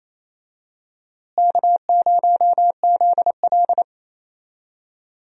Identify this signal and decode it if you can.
Morse code, heard as a keyed tone, about 27 wpm, 700 Hz. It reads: K0ZL
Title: USA Call Signs - 4 Characters - Repeated 2x - 28wpm